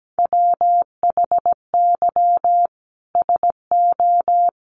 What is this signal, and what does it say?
Morse code, heard as a keyed tone, about 17 wpm, 700 Hz. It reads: WHY SO
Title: Sentences from Top 1500 Words 17wpm